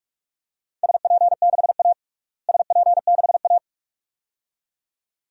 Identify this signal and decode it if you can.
Morse code, heard as a keyed tone, about 46 wpm, 700 Hz. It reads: SP6A SP6A
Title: DXCC-100 Call Signs - 4 Characters - Repeated 2x - 45wpm